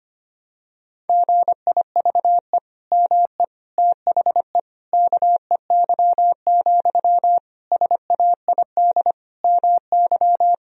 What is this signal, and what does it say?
Morse code, heard as a keyed tone, about 25 wpm, 700 Hz. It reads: GIVE ME THE KEY, SAID MY
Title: Sentences from Top 1000 Words - Review Part 4 - 25wpm